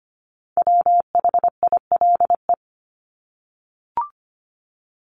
Morse code, heard as a keyed tone, about 25 wpm, 700 Hz. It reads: WHILE
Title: Top 300 Words Repeated 25wpm